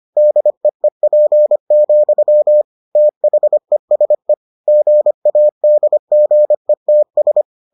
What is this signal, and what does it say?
Morse code, heard as a keyed tone, about 25 wpm, 600 Hz. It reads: DEEP, THESE GADGETS